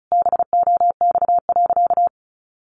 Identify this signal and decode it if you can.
Morse code, heard as a keyed tone, about 34 wpm, 700 Hz. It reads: BOX.
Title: Sentences from Top 1000 Words 35wpm